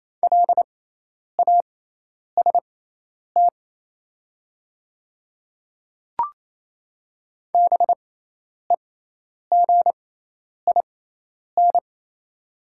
Morse code, heard as a keyed tone, about 28 wpm, spaced as for 10 wpm, 690 Hz. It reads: LAST BEGIN
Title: 6x Character Spacing - Top 300 Words - 28wpm